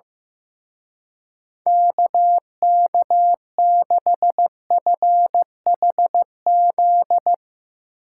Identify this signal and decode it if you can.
Morse code, heard as a keyed tone, about 15 wpm, 700 Hz. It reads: KK6FHZ
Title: Speed Racing - USA 2x3 Call Signs 15wpm